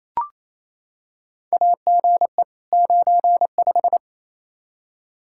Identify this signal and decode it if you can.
Morse code, heard as a keyed tone, about 28 wpm, 700 Hz. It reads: AGE 95
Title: K1USN SST QSO Practice 28wpm